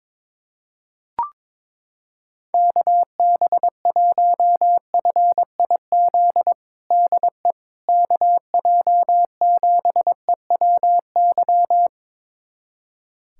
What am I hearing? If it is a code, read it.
Morse code, heard as a keyed tone, about 22 wpm, 700 Hz. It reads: KB1FIZ DE KJ7EWY
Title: USA Callsign DE Callsign 22wpm